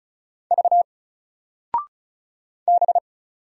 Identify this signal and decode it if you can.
Morse code, heard as a keyed tone, about 35 wpm, 700 Hz. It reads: V B
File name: Single-Letter-Number-35wpm